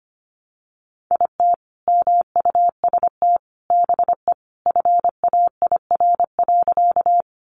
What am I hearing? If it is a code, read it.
Morse code, heard as a keyed tone, about 25 wpm, 700 Hz. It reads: IT MUST BE FAIR.
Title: Sentences from Top 1300 Words 25wpm